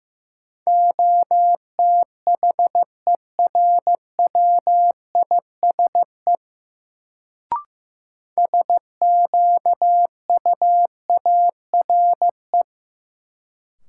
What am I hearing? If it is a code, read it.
Morse code, heard as a keyed tone, about 15 wpm, 700 Hz. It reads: OTHERWISE SQUARE
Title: Sentences from Top 1200 Words - New Word Drill Alt - 15wpm